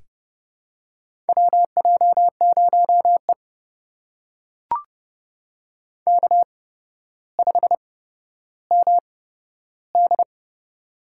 Morse code, heard as a keyed tone, about 30 wpm, spaced as for 13 wpm, 700 Hz. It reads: WJ0E K5MD
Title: USA Call Signs - 4 Characters - 8x Character Spacing - 30wpm